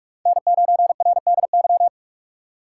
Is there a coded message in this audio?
Yes: Morse code from 0.3 to 1.9 s, heard as a keyed tone, about 43 wpm, 700 Hz. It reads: N9RDY